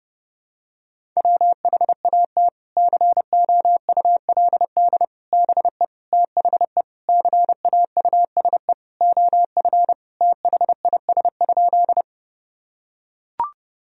Morse code, heard as a keyed tone, about 31 wpm, 700 Hz. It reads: WHAT COULD BE THE CAUSE OF THIS?
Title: Sentences from Top 1000 Words - Review Part 1 - 30wpm